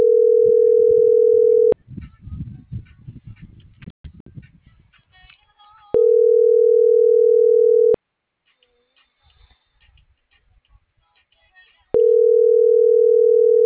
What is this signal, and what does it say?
Ringback tone (North America), 440 + 480 Hz.